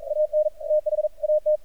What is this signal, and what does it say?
Morse code, heard as a keyed tone, about 43 wpm, 610 Hz. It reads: AN THAT
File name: archived / CRSnet